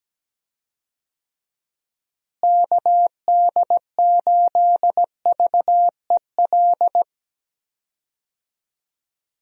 Morse code, heard as a keyed tone, about 17 wpm, 700 Hz. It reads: KD8VEL
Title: USA Call Signs - 6 Characters - Repeated 3x - 17wpm